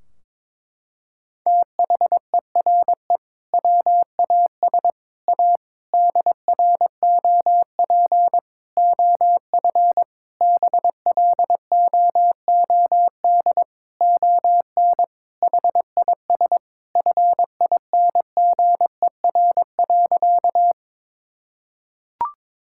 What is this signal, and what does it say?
Morse code, heard as a keyed tone, about 21 wpm, 700 Hz. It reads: THERE WAS A DROP OF BLOOD ON HIS FINGER.
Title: Sentences from Top 1600 Words 22wpm